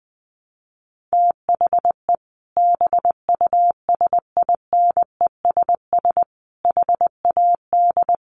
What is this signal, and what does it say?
Morse code, heard as a keyed tone, about 20 wpm, 700 Hz. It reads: THE BUSINESS HAD
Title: Sentences from Top 1400 Words 20wpm